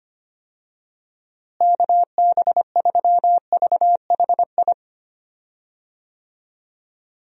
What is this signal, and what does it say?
Morse code, heard as a keyed tone, about 25 wpm, 700 Hz. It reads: KB3VHI
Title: USA Call Signs - 6 Characters - Repeated 2x - 25wpm